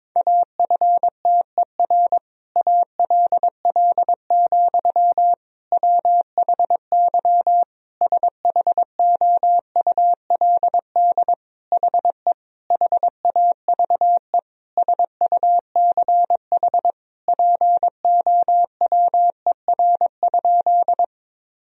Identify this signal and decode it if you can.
Morse code, heard as a keyed tone, about 22 wpm, 700 Hz. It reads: AFTER ALL, WHY SHOULD HE HAVE SUCH POWER?
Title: Sentences from Top 1000 Words - Review Part 2 - 22wpm